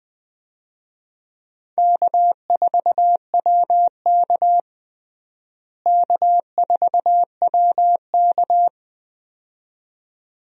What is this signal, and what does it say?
Morse code, heard as a keyed tone, about 20 wpm, 700 Hz. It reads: K4WK K4WK
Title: USA Call Signs - 4 Characters - Repeated 2x - 20wpm